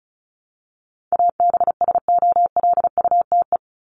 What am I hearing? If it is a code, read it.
Morse code, heard as a keyed tone, about 34 wpm, 700 Hz. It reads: ABSOLUTE
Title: LY Suffix - Top 100 Words with Base Words 35wpm